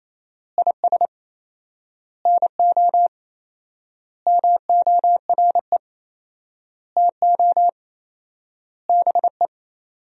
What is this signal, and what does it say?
Morse code, heard as a keyed tone, about 28 wpm, 700 Hz. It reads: IS NO MORE TO BE
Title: 4x Word-Spacing - Sentences from Top 100 Words - 28wpm